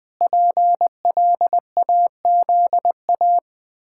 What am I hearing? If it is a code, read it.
Morse code, heard as a keyed tone, about 20 wpm, 700 Hz. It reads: PLAZA
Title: Pangram Challenge v2 - 20wpm